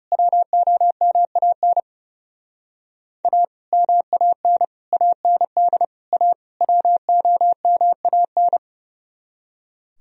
Morse code, heard as a keyed tone, about 36 wpm, 700 Hz. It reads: WOMAN A MAN AND A WOMAN
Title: Speed-Racing - Sets of 5 Words 30wpm